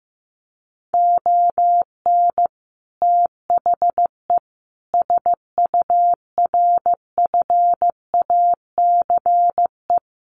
Morse code, heard as a keyed tone, about 15 wpm, 700 Hz. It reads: ON THE SURFACE
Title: Sets of 5 Words Encore 15wpm